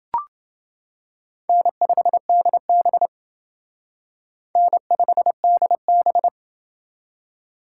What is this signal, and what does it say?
Morse code, heard as a keyed tone, about 28 wpm, 690 Hz. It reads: N5DB N5DB